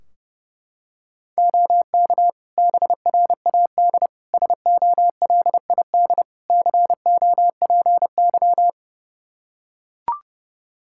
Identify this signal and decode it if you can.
Morse code, heard as a keyed tone, about 30 wpm, 700 Hz. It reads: OK BRAD SOLID COPY
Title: K1USN SST QSO Practice 30wpm